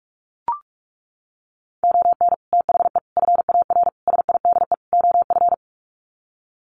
Morse code, heard as a keyed tone, about 44 wpm, 700 Hz. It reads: ON THE FAR SIDE OF
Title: Sets of 5 Words 45wpm